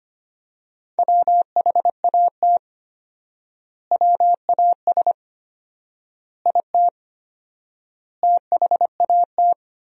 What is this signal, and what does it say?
Morse code, heard as a keyed tone, about 25 wpm, 700 Hz. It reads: WHAT WAS IT THAT
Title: 4x Word-Spacing - Sentences from Top 300 Words - 25wpm